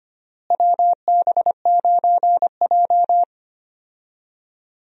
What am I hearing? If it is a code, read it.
Morse code, heard as a keyed tone, about 25 wpm, 700 Hz. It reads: WB9J
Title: USA Call Signs - 4 Characters - Repeated 3x - 25wpm